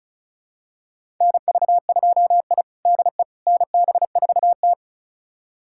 Morse code, heard as a keyed tone, about 36 wpm, 700 Hz. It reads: NV2I DE NB4T
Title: USA Callsign DE Callsign - 4 Characters - 35wpm